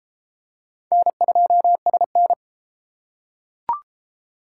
Morse code, heard as a keyed tone, about 33 wpm, 700 Hz. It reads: N2SN